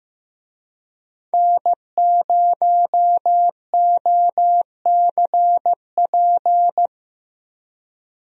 Morse code, heard as a keyed tone, about 15 wpm, 700 Hz. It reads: N0OCP
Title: USA Call Signs - 5 Characters - Repeated 3x - 15wpm